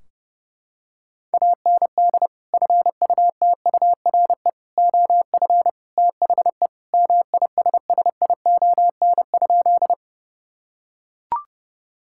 Morse code, heard as a keyed tone, about 30 wpm, 700 Hz. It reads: AND FUTURE OF THE MISSION?
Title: Sentences from Top 1600 Words 30wpm